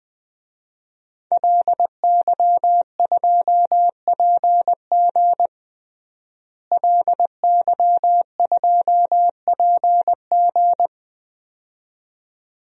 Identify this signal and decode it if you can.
Morse code, heard as a keyed tone, about 20 wpm, 700 Hz. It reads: LY2PG LY2PG
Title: DXCC-100 Call Signs - 5 Characters - Repeated 2x - 20wpm